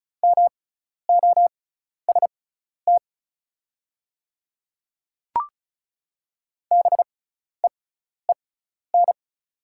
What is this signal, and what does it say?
Morse code, heard as a keyed tone, about 33 wpm, spaced as for 12 wpm, 690 Hz. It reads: MOST BEEN
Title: 6x Character Spacing - Top 100 Words - 35wpm